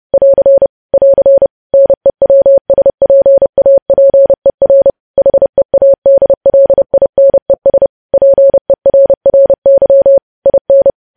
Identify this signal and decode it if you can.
Morse code, heard as a keyed tone, about 30 wpm, 550 Hz. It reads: <AR> <AR> NEWSPAPER HEADLINES PERRY IN
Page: news headlines in morse code at 30 wpm